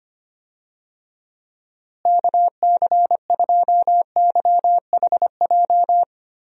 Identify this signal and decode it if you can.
Morse code, heard as a keyed tone, about 25 wpm, 700 Hz. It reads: KC2YHJ
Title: USA Call Signs - 6 Characters - Repeated 3x - 25wpm